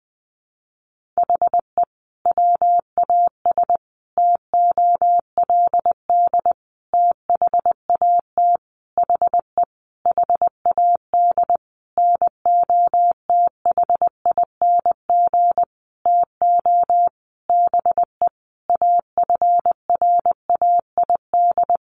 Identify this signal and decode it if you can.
Morse code, heard as a keyed tone, about 20 wpm, 700 Hz. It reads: HE WAS TOLD THAT HE HAD NOTHING TO BE AFRAID
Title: Sentences from Top 1500 Words 20wpm